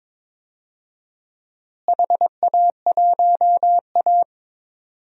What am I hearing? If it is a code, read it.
Morse code, heard as a keyed tone, about 21 wpm, 700 Hz. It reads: HA1A
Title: DXCC-100 Call Signs - 4 Characters - Repeated 2x - 22wpm